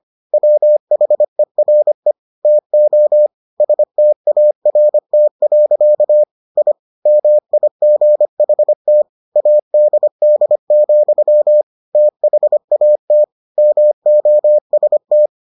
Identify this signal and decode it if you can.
Morse code, heard as a keyed tone, about 25 wpm, 600 Hz. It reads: WHERE TO START. I MIGHT ADD, THAT MOST